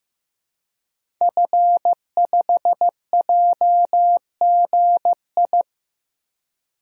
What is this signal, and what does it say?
Morse code, heard as a keyed tone, about 15 wpm, 700 Hz. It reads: F5JGI
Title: DXCC-100 Call Signs - 5 Characters - Repeated 2x - 15wpm